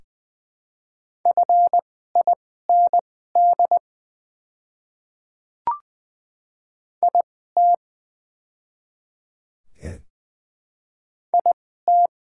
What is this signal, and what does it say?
Morse code, heard as a keyed tone, about 20 wpm, spaced as for 15 wpm, 700 Hz. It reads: FIND IT IT
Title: 2x Character Spacing - Top 100 Words - 20wpm